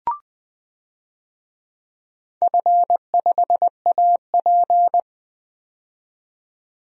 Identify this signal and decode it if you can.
Morse code, heard as a keyed tone, about 20 wpm, 700 Hz. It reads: F5AP
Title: DXCC-100 Call Signs - 4 Characters - 20wpm